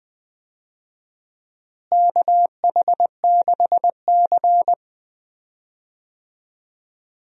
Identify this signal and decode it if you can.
Morse code, heard as a keyed tone, about 21 wpm, 700 Hz. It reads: KH6C
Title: DXCC-100 Call Signs - 4 Characters - Repeated 3x - 20wpm